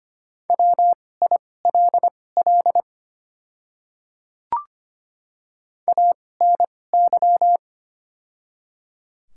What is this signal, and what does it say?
Morse code, heard as a keyed tone, about 25 wpm, spaced as for 18 wpm, 700 Hz. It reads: WILL ANY
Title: 2x Character Spacing - Top 200 Words - 25wpm